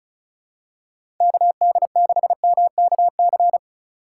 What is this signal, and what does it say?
Morse code, heard as a keyed tone, about 36 wpm, 700 Hz. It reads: KD6MKC
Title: USA Call Signs - 6 Characters - 2x Character Spacing - 35wpm